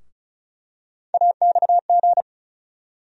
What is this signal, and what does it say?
Morse code, heard as a keyed tone, about 36 wpm, 700 Hz. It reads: AXG
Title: USA Call Sign Postfixes - 3 Letters 35wpm